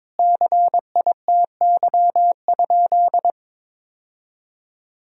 Morse code, heard as a keyed tone, about 22 wpm, 700 Hz. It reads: CITY?
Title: 4x Word-Spacing - Sentences from Top 300 Words - 22wpm